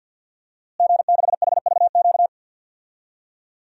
Morse code, heard as a keyed tone, about 50 wpm, 700 Hz. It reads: G6HVQ
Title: DXCC-100 Call Signs - 5 Characters - Repeated 3x - 50wpm